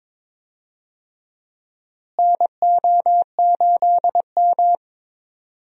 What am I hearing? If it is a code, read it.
Morse code, heard as a keyed tone, about 22 wpm, 700 Hz. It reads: NO8M
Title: USA Call Signs - 4 Characters - Repeated 2x - 22wpm